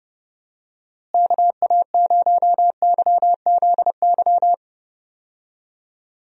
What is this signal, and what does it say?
Morse code, heard as a keyed tone, about 30 wpm, 700 Hz. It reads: KA0YZY
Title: USA Call Signs - 6 Characters - Repeated 3x - 30wpm